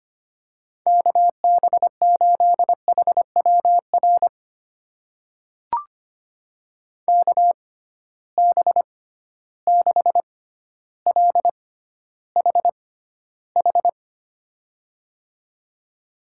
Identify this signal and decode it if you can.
Morse code, heard as a keyed tone, about 25 wpm, spaced as for 13 wpm, 700 Hz. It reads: KB8HWR KB6LHH
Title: USA Call Signs - 6 Characters - 6x Character Spacing - 25wpm